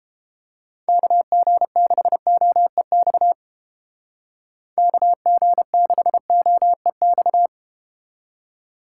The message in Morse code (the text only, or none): KG6OEX KG6OEX